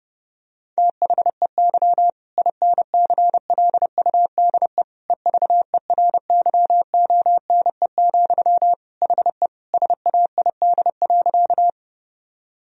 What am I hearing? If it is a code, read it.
Morse code, heard as a keyed tone, about 32 wpm, 700 Hz. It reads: THEY INCLUDE EVERYONE, HE SAID.
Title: Sentences from Top 1000 Words - Review Part 3 - 30wpm